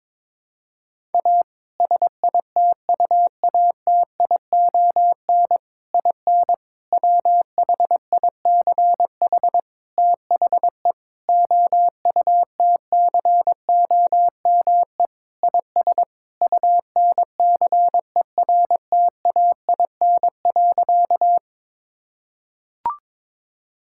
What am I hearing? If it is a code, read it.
Morse code, heard as a keyed tone, about 22 wpm, 700 Hz. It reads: A SITUATION IN WHICH THE OUTCOME IS UNCERTAIN.
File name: practice / 22wpm / Sentences with 3-Word Phrases - 22wpm